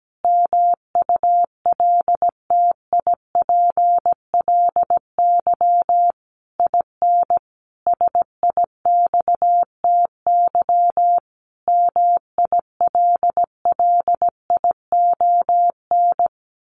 Morse code, heard as a keyed tone, about 17 wpm, 700 Hz. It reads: MULTIPLY IN SIXTY MILLION
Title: A Sound Of Thunder 17wpm